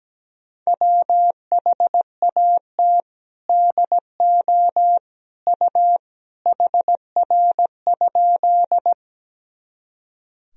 Morse code, heard as a keyed tone, about 17 wpm, 700 Hz. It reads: WHAT DO U HR?
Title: Sentences from Top 300 Words and Abbreviations 17wpm